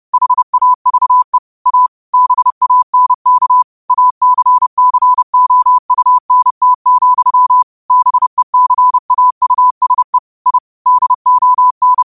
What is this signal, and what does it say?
Morse code, heard as a keyed tone, about 30 wpm, 1000 Hz. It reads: HAVE A BANK ACCOUNT, BECAUSE I DON
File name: qotdfiles / 30wpm